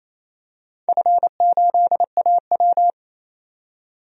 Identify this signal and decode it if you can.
Morse code, heard as a keyed tone, about 29 wpm, 690 Hz. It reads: F8AW
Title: DXCC-100 Call Signs - 4 Characters - 28wpm